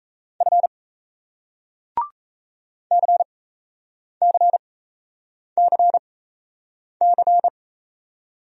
Morse code, heard as a keyed tone, about 40 wpm, spaced as for 8 wpm, 700 Hz. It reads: F CCCC